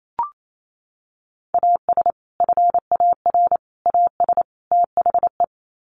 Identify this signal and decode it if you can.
Morse code, heard as a keyed tone, about 28 wpm, 700 Hz. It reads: AS FAR AS THE
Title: Sets of 4 Words 28wpm